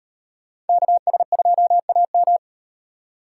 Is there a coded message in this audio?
Yes: Morse code from 0.7 to 2.4 s, heard as a keyed tone, about 36 wpm, 700 Hz. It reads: KS2AM